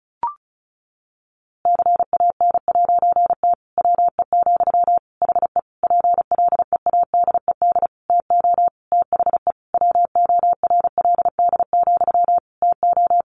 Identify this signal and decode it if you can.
Morse code, heard as a keyed tone, about 34 wpm, 700 Hz. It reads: CAN'T WE, HE PLEADED TO THE WORLD, TO
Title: A Sound Of Thunder 35wpm